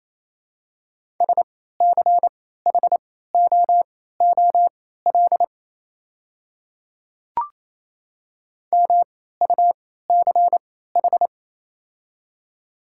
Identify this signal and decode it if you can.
Morse code, heard as a keyed tone, about 28 wpm, spaced as for 16 wpm, 700 Hz. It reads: SCHOOL MUCH
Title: 3x Character Spacing - Top 500 Words - 28wpm